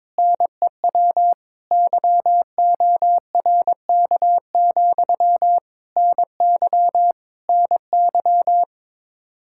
Morse code, heard as a keyed tone, about 22 wpm, 700 Hz. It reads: NEW YORK, NY NY
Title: Practice QSO - Exchange 1 - Beginning - 22wpm